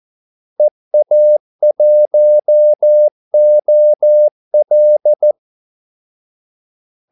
Morse code, heard as a keyed tone, about 14 wpm, 600 Hz. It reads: EA1OL
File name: Callsigns-14wpm